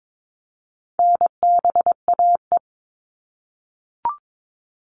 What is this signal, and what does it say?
Morse code, heard as a keyed tone, about 22 wpm, 700 Hz. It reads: NBAE
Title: All Cut Numbers - 4-Digit 22wpm